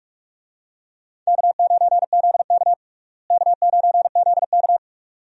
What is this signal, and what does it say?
Morse code, heard as a keyed tone, about 43 wpm, 700 Hz. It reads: K9ZK K9ZK
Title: USA Call Signs - 4 Characters - Repeated 2x - 45wpm